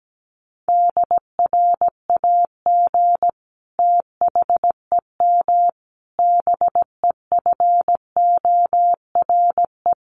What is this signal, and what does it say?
Morse code, heard as a keyed tone, about 17 wpm, 700 Hz. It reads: DRAG THEM BEFORE
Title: Sentences from Top 1500 Words 17wpm